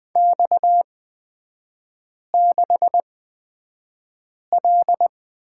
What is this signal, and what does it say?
Morse code, heard as a keyed tone, about 20 wpm, spaced as for 5 wpm, 690 Hz. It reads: X6L
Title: 229 Lesson 37e - Letter x - Three Characters 20wpm